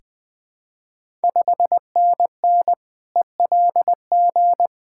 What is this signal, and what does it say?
Morse code, heard as a keyed tone, about 20 wpm, 700 Hz. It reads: 5NN ELG
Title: Ontario QSO Party Exchange 20wpm